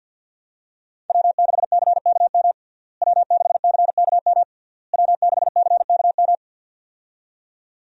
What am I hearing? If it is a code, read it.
Morse code, heard as a keyed tone, about 52 wpm, 700 Hz. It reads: W6CKM W6CKM W6CKM
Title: USA Call Signs - 5 Characters - Repeated 3x - 50wpm